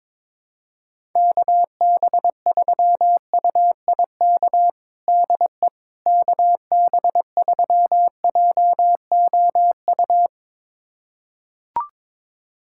Morse code, heard as a keyed tone, about 22 wpm, 700 Hz. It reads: KB3UIK DE KB3JOU
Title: USA Callsign DE Callsign - 6 Characters - 22wpm